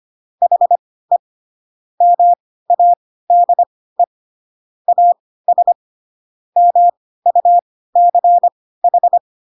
Morse code, heard as a keyed tone, about 25 wpm, spaced as for 16 wpm, 700 Hz. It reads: HE MADE AS MUCH
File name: Pinocchio / Pinocchio ch7 10 25 ik1hge